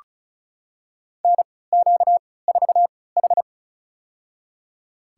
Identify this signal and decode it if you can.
Morse code, heard as a keyed tone, about 34 wpm, spaced as for 20 wpm, 700 Hz. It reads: NQ4H
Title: USA Call Signs - 4 Characters - 3x Character Spacing - 35wpm